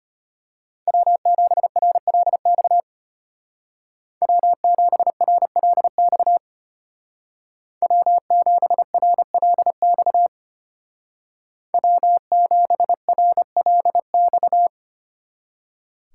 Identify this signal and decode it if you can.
Morse code, heard as a keyed tone, about 33 wpm, 700 Hz. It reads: W7RLX W7RLX W7RLX W7RLX